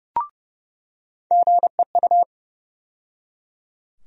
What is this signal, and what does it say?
Morse code, heard as a keyed tone, about 30 wpm, 700 Hz. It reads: GEU
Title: USA Call Sign Postfixes - 3 Letters 30wpm